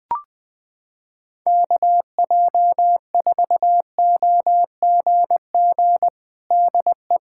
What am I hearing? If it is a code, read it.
Morse code, heard as a keyed tone, about 20 wpm, 700 Hz. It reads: KJ4OGG DE
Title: USA Callsign DE Callsign - 6 Characters - 20wpm